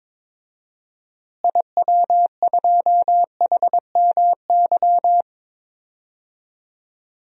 Morse code, heard as a keyed tone, about 22 wpm, 700 Hz. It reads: IW2HMY
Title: DXCC-100 Call Signs - 6 Characters - Repeated 2x - 22wpm